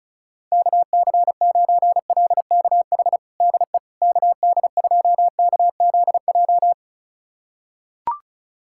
Morse code, heard as a keyed tone, about 36 wpm, 700 Hz. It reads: KC9LKH DE KD2KZJ
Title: USA Callsign DE Callsign 35wpm